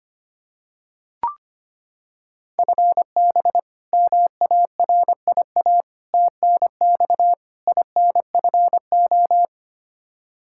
Morse code, heard as a keyed tone, about 25 wpm, 700 Hz. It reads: FB MARIA TNX INFO
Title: Practice QSO - Exchange 3 - More Info - 25wpm